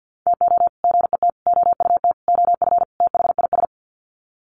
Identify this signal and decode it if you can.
Morse code, heard as a keyed tone, about 48 wpm, 700 Hz. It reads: TO GET OUT OF THIS
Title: Sets of 5 Words Encore2 50wpm